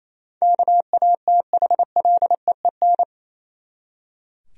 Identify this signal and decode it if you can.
Morse code, heard as a keyed tone, about 27 wpm, 700 Hz. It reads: KATHLEEN
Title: USA Common Names - 28wpm